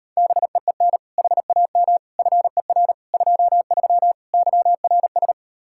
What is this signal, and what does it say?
Morse code, heard as a keyed tone, about 33 wpm, 700 Hz. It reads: BEEN HAM FER 23 YRS